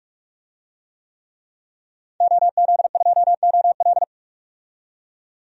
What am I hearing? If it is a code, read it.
Morse code, heard as a keyed tone, about 47 wpm, 700 Hz. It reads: OZ2OL
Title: DXCC-100 Call Signs - 5 Characters - 45wpm